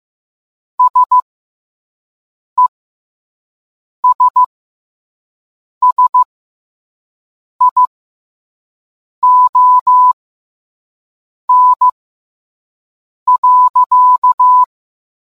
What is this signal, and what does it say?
Morse code, heard as a keyed tone, about 15 wpm, spaced as for 5 wpm, 1000 Hz. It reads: SESSION.